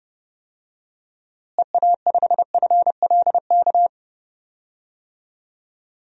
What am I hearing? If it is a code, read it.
Morse code, heard as a keyed tone, about 30 wpm, 700 Hz. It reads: EA5FLK
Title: DXCC-100 Call Signs - 6 Characters - Repeated 3x - 30wpm